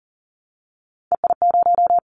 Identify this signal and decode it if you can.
Morse code, heard as a keyed tone, about 42 wpm, 700 Hz. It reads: EI0
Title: Callsign Prefixes for Top 50 DXCC - 40wpm